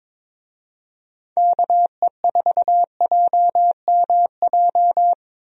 Morse code, heard as a keyed tone, about 22 wpm, 700 Hz. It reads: KE4JMJ
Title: USA Call Signs - 6 Characters - Repeated 2x - 22wpm